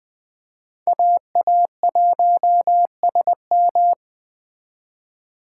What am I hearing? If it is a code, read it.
Morse code, heard as a keyed tone, about 20 wpm, 700 Hz. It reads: AA1SM